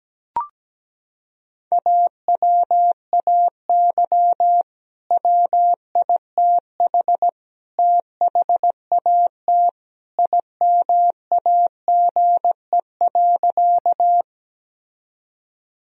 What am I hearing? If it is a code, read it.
Morse code, heard as a keyed tone, about 17 wpm, 700 Hz. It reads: AWAY WITH THAT IMAGE.
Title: Sentences from Top 1000 Words - Review Part 4 - 17wpm